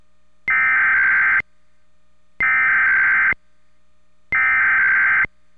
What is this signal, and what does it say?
Emergency Alert System (SAME) tones (header bursts), heard as harsh buzzing bursts. It is heard from 0.5 to 5.2 s.
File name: WHYY